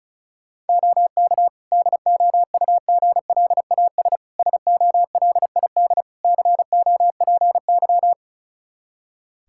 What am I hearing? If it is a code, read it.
Morse code, heard as a keyed tone, about 36 wpm, 700 Hz. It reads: OK DOUGLAS SOLID COPY
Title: Practice QSO - Exchange 3 - More Info - 35wpm